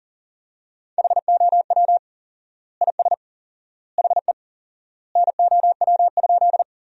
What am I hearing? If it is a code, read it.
Morse code, heard as a keyed tone, about 40 wpm, 700 Hz. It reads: HOW IS HE NOW?
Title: 4x Word-Spacing - Sentences from Top 100 Words - 40wpm